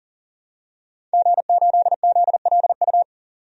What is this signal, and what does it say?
Morse code, heard as a keyed tone, about 37 wpm, 700 Hz. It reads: G8ZLU
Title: DXCC-100 Call Signs - 5 Characters - 40wpm